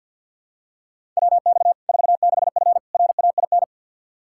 Morse code, heard as a keyed tone, about 53 wpm, 700 Hz. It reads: WX 46F RAIN
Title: Practice QSO - Exchange 2 - Info - 50wpm